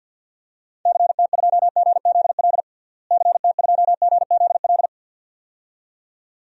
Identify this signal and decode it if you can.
Morse code, heard as a keyed tone, about 46 wpm, 700 Hz. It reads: CT2GZL CT2GZL
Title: DXCC-100 Call Signs - 6 Characters - Repeated 2x - 50wpm